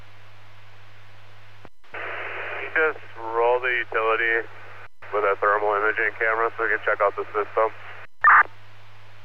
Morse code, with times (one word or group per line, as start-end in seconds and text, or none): none